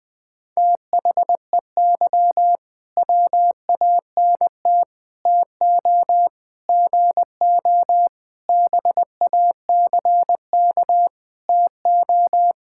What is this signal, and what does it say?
Morse code, heard as a keyed tone, about 20 wpm, 700 Hz. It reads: THEY WANT TO GO BACK TO
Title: Sentences with 5-Word Phrases - 20wpm